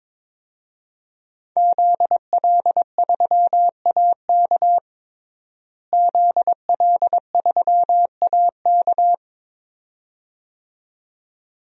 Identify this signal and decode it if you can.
Morse code, heard as a keyed tone, about 21 wpm, 700 Hz. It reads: ZL3AK ZL3AK
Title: DXCC-100 Call Signs - 5 Characters - Repeated 2x - 22wpm